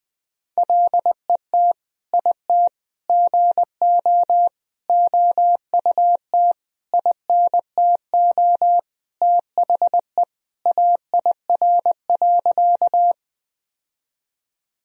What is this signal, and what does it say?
Morse code, heard as a keyed tone, about 20 wpm, 700 Hz. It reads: LET IT GO OUT INTO THE AIR.
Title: Sentences from Top 1000 Words - Review Part 1 - 20wpm